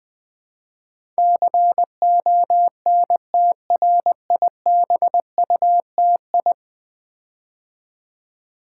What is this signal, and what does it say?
Morse code, heard as a keyed tone, about 20 wpm, 700 Hz. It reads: CONTRIBUTI
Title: Build A Word - 5 Letters At A Time - 12-Letter Words - 20wpm